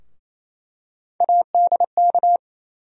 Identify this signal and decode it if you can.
Morse code, heard as a keyed tone, about 28 wpm, 700 Hz. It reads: ADK
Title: USA Call Sign Postfixes - 3 Letters 28wpm